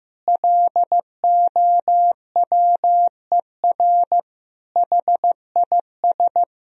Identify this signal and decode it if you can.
Morse code, heard as a keyed tone, about 15 wpm, 700 Hz. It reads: LOWER HIS
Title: Sentences from Top 1000 Words - Review Part 4 - 15wpm